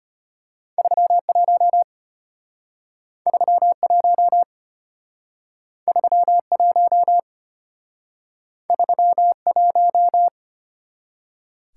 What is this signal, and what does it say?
Morse code, heard as a keyed tone, about 32 wpm, 700 Hz. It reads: 31 31 31 31